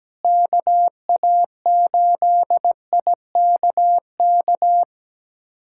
Morse code, heard as a keyed tone, about 17 wpm, 700 Hz. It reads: KA8IKK
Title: USA Callsign DE Callsign - 6 Characters - 17wpm